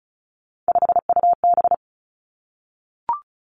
Morse code, heard as a keyed tone, about 36 wpm, 700 Hz. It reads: 5UB
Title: Three Challenging Letter-Number Combinations 35wpm